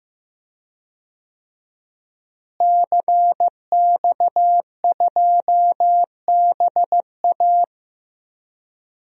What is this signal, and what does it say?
Morse code, heard as a keyed tone, about 15 wpm, 700 Hz. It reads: CX2BA
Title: DXCC-100 Call Signs - 5 Characters - Repeated 2x - 15wpm